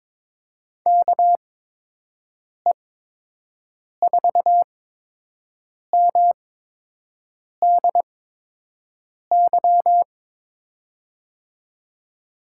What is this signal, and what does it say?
Morse code, heard as a keyed tone, about 23 wpm, spaced as for 6 wpm, 700 Hz. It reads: KE4MDY
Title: USA Call Signs - 6 Characters - 8x Character Spacing - 22wpm